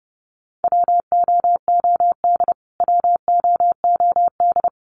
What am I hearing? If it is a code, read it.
Morse code, heard as a keyed tone, about 30 wpm, 700 Hz. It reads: WOOD WOOD
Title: Top 500 Words Repeated 30wpm